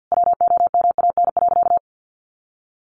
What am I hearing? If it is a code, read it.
Morse code, heard as a keyed tone, about 50 wpm, 700 Hz. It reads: WOMAN.
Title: Sentences from Top 1100 Words 50wpm